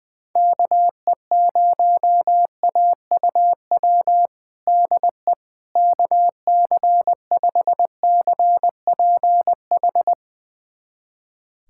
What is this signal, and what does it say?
Morse code, heard as a keyed tone, about 20 wpm, 700 Hz. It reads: KE0AUW DE KC5CPH
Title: USA Callsign DE Callsign - 6 Characters - 20wpm